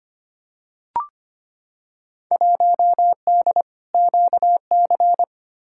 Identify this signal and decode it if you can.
Morse code, heard as a keyed tone, about 25 wpm, 700 Hz. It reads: 1D QC
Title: Field Day Exchanges - 25wpm